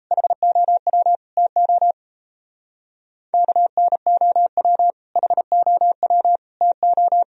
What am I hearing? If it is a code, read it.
Morse code, heard as a keyed tone, about 35 wpm, 700 Hz. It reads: HOW TO KNOW HOW TO